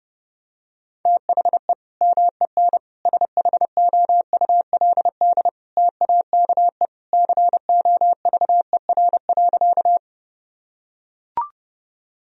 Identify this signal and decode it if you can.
Morse code, heard as a keyed tone, about 32 wpm, 700 Hz. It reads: THE MEN SHOULD TAKE COVER.
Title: Sentences from Top 1000 Words - Review Part 1 - 30wpm